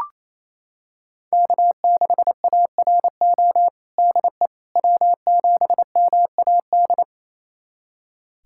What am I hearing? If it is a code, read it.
Morse code, heard as a keyed tone, about 28 wpm, 700 Hz. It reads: K6ARO DE W7MAD
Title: USA Callsign DE Callsign - 5 Characters - 28wpm